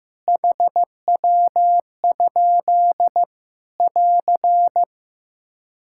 Morse code, heard as a keyed tone, about 16 wpm, 700 Hz. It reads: HW? <AR>
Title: K1USN SST QSO Practice 15wpm